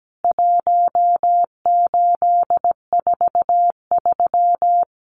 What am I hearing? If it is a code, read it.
Morse code, heard as a keyed tone, about 17 wpm, 700 Hz. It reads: 1843
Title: CWT Exchange 17wpm